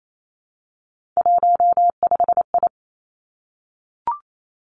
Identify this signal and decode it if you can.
Morse code, heard as a keyed tone, about 28 wpm, 700 Hz. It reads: 15I
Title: Three Challenging Letter-Number Combinations 28wpm